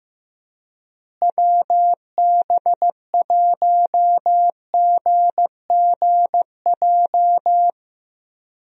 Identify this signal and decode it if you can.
Morse code, heard as a keyed tone, about 15 wpm, 700 Hz. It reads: WB1GGJ
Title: USA Call Signs - 6 Characters - Repeated 2x - 15wpm